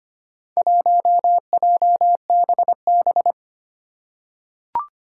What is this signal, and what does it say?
Morse code, heard as a keyed tone, about 25 wpm, 700 Hz. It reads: 1JBB
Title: Four Challenging Letter-Number Combinations 25wpm